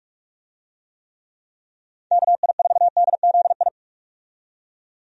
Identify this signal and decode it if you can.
Morse code, heard as a keyed tone, about 41 wpm, 700 Hz. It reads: KI4DZI
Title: USA Call Signs - 6 Characters - 45wpm